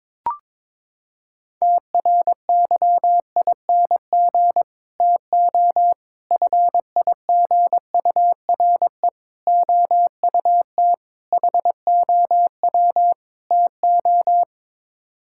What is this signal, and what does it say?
Morse code, heard as a keyed tone, about 22 wpm, 700 Hz. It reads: TRYING TO FIGURE OUT HOW TO
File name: practice / 22wpm / Sets of 6 Words - 22wpm